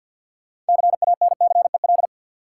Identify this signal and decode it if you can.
Morse code, heard as a keyed tone, about 52 wpm, 700 Hz. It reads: CANCEL
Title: Sentences from Top 1200 Words - New Word Drill Alt - 50wpm